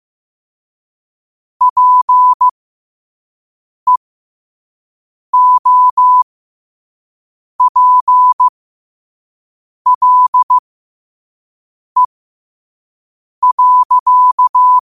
Morse code, heard as a keyed tone, about 15 wpm, spaced as for 5 wpm, 1000 Hz. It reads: PEOPLE.
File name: qotdfiles / 5wpm